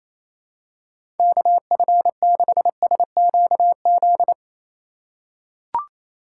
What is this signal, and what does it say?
Morse code, heard as a keyed tone, about 28 wpm, 700 Hz. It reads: KF6SQZ
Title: USA Call Signs - 3x Character Spacing - 28wpm